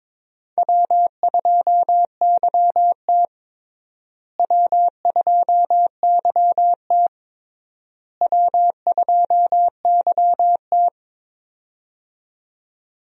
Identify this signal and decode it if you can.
Morse code, heard as a keyed tone, about 22 wpm, 700 Hz. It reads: W2YT W2YT W2YT
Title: USA Call Signs - 4 Characters - Repeated 3x - 22wpm